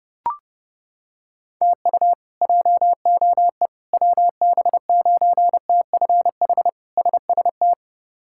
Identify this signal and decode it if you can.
Morse code, heard as a keyed tone, about 30 wpm, 700 Hz. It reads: TU JOE WB9TFH SST
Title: K1USN SST QSO Practice - 30wpm